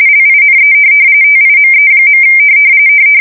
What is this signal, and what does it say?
RTTY, about 45.45 baud. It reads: LINE TIME.
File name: fsk170 / printing-telegraph